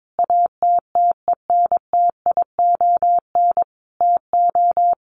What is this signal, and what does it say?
Morse code, heard as a keyed tone, about 22 wpm, 700 Hz. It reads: ATTENTION TO
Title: Sentences from Top 1000 Words 22wpm